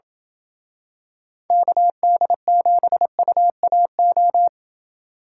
Morse code, heard as a keyed tone, about 28 wpm, 700 Hz. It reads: KD7UAO